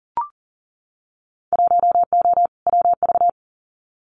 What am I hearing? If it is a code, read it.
Morse code, heard as a keyed tone, about 39 wpm, 700 Hz. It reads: 1O WV
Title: Winter Field Day Exchanges 40wpm